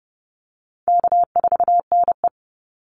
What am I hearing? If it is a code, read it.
Morse code, heard as a keyed tone, about 30 wpm, 700 Hz. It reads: K4NE
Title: Easy Call Signs - 30wpm